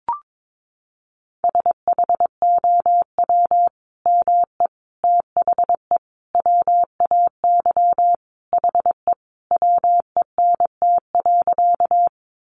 Morse code, heard as a keyed tone, about 22 wpm, 700 Hz. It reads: SHOW ME THE WAY HE WENT.
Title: Sentences from Top 200 Words 22wpm